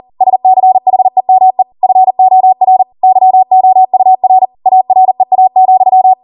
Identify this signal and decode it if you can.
Morse code, heard as a keyed tone, about 40 wpm, 750 Hz. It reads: SCHEME FOR YOUR AREA,
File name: Morse / 40WPM